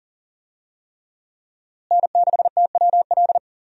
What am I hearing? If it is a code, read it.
Morse code, heard as a keyed tone, about 40 wpm, 700 Hz. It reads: N6TWL
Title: USA Call Signs - 5 Characters - Repeated 3x - 40wpm